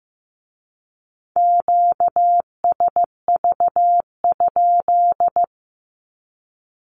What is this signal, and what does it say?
Morse code, heard as a keyed tone, about 15 wpm, 700 Hz. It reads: QSV?
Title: Q-Codes with Spoken Meaning - 15wpm